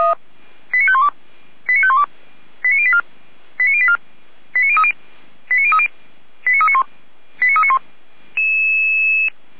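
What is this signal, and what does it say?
Five-tone selective call (ZVEI), heard as quick runs of five tones, tones of about 70 ms. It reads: 89512 89512 89084 89084 89030 89030 89391 89391
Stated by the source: ZVEI five-tone call received via SDR, from the Feuerwehr Orlamünde audio server.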